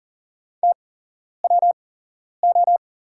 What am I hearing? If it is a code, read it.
Morse code, heard as a keyed tone, about 42 wpm, spaced as for 11 wpm, 700 Hz. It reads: TWO
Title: 8x Character Spacing - Top 200 Words - 40wpm